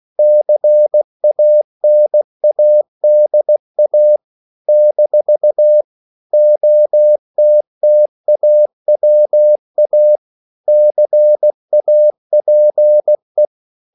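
Morse code, heard as a keyed tone, about 16 wpm, 600 Hz. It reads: CANADA - OTTAWA CAPE
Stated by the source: CW practice of national capitals at 16 words per minute, from ON6ZQ.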